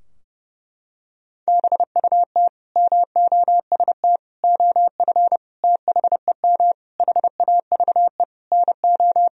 Morse code, heard as a keyed tone, about 31 wpm, 700 Hz. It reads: BUT MOST OF THEM HAVE NO
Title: Sentences from Top 1000 Words - Review Part 3 - 30wpm